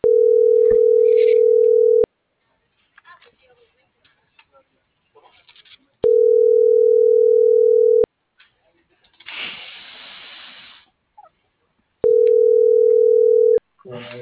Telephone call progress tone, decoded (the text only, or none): ringback tone